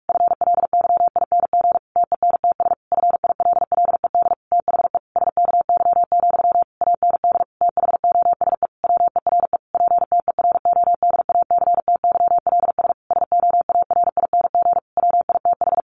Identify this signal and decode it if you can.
Morse code, heard as a keyed tone, about 44 wpm, 700 Hz. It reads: FLYING TENTS FILLED THE SKY, AND THOSE WERE PTERODACTYLS SOARING WITH
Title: A Sound Of Thunder 45wpm